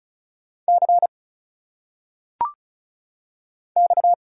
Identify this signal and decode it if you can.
Morse code, heard as a keyed tone, about 33 wpm, 700 Hz. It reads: C X
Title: Single Letters - Phonetic Pronunciation 35wpm